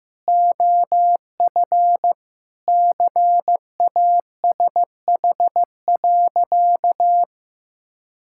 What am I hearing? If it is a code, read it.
Morse code, heard as a keyed tone, about 15 wpm, 700 Hz. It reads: OF CASH.
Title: Sentences from Top 1000 Words - Review Part 5 - 15wpm